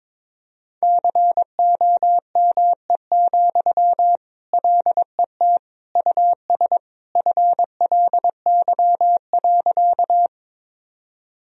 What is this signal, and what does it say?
Morse code, heard as a keyed tone, about 22 wpm, 700 Hz. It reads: COME, LET US FLY.
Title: Sentences from Top 1000 Words - Review Part 2 - 22wpm